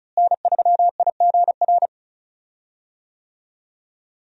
Morse code, heard as a keyed tone, about 35 wpm, 700 Hz. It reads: N3IGR
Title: USA Call Signs - 5 Characters - Repeated 3x - 35wpm